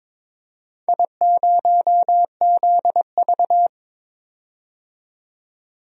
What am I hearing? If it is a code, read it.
Morse code, heard as a keyed tone, about 22 wpm, 700 Hz. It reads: I0ZV
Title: DXCC-100 Call Signs - 4 Characters - 22wpm